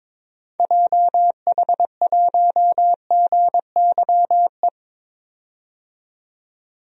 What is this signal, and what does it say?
Morse code, heard as a keyed tone, about 22 wpm, 700 Hz. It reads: JH1GYE
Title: DXCC-100 Call Signs - 6 Characters - 22wpm